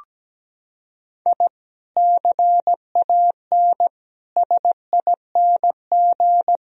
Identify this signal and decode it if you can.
Morse code, heard as a keyed tone, about 17 wpm, 700 Hz. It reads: I CAN SING
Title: Sentences from Top 1000 Words - Review Part 2 - 17wpm